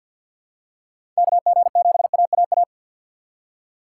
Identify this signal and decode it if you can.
Morse code, heard as a keyed tone, about 47 wpm, 700 Hz. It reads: KG7AAA
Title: USA Call Signs - 2x Character Spacing - 50wpm